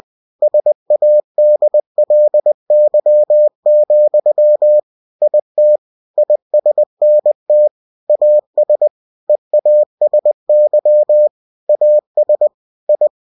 Morse code, heard as a keyed tone, about 20 wpm, 600 Hz. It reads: SADLY, IT ISNT AS EASY AS I
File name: cw-books / winnie-the-pooh / winnie-20wpm-page-0036